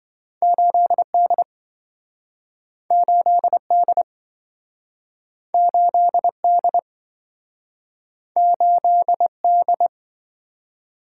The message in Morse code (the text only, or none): 8D 8D 8D 8D